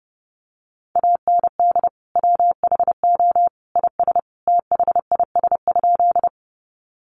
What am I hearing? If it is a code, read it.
Morse code, heard as a keyed tone, about 30 wpm, 700 Hz. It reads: AND WHO IS THIS?
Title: Sentences from Top 100 Words 30wpm